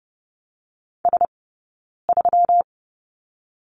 Morse code, heard as a keyed tone, about 30 wpm, 700 Hz. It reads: S 3
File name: Lesson 20d - Number 3 - Two Characters 30wpm